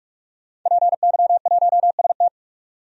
Morse code, heard as a keyed tone, about 46 wpm, 700 Hz. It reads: PY1ST
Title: DXCC-100 Call Signs - 5 Characters - 45wpm